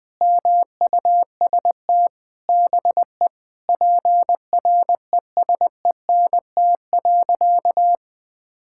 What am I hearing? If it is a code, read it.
Morse code, heard as a keyed tone, about 19 wpm, 700 Hz. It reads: MUST BE PRESENT.
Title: Sentences from Top 1600 Words 20wpm